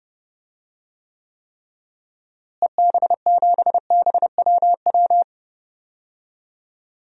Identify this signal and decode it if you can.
Morse code, heard as a keyed tone, about 30 wpm, 700 Hz. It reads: EB7BWW
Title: DXCC-100 Call Signs - 6 Characters - Repeated 3x - 30wpm